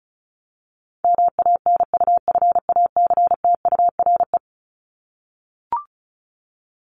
Morse code, heard as a keyed tone, about 34 wpm, 700 Hz. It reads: MANUFACTURE
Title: ING Suffix - Top 100 Words with Base Words 35wpm